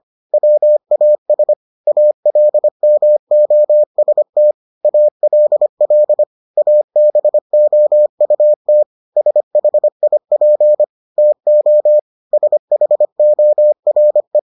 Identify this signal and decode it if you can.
Morse code, heard as a keyed tone, about 25 wpm, 600 Hz. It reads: WAS ALMOST ALL ABOUT SHIP TO SHORE